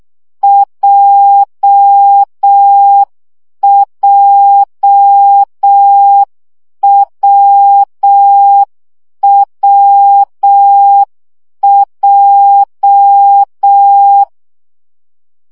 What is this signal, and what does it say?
Morse code, heard as a keyed tone, about 6 wpm, 800 Hz. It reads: JJWWJ
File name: CW gyakorlas w p j 5 baud